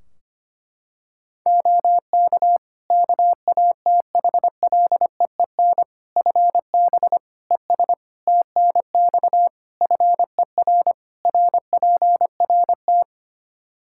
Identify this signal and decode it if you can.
Morse code, heard as a keyed tone, about 25 wpm, 700 Hz. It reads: OK KATHLEEN FB ES TNX FER RPRT
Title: Practice QSO - Exchange 2 - Info - 25wpm